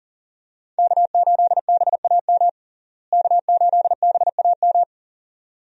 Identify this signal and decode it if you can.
Morse code, heard as a keyed tone, about 40 wpm, 700 Hz. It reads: K8BAM K8BAM
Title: USA Call Signs - 5 Characters - Repeated 2x - 40wpm